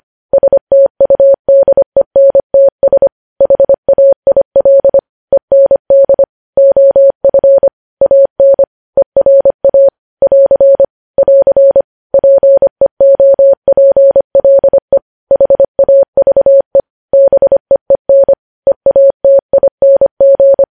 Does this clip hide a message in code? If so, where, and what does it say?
Yes: Morse code from 0.3 to 20.6 s, heard as a keyed tone, about 25 wpm, 550 Hz. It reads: STUDENTS HAIL END OF AN ERA <AR> <AR> PEOPLE HAVE BEEN EATING